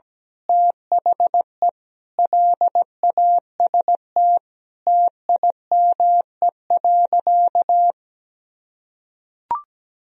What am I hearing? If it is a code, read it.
Morse code, heard as a keyed tone, about 17 wpm, 700 Hz. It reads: THE LAST TIME.
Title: Sentences from Top 1000 Words - Review Part 2 - 17wpm